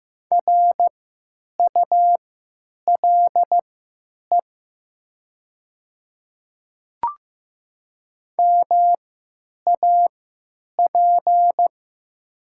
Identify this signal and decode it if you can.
Morse code, heard as a keyed tone, about 15 wpm, spaced as for 9 wpm, 700 Hz. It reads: RULE MAP
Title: 3x Character Spacing - Top 500 Words - 15wpm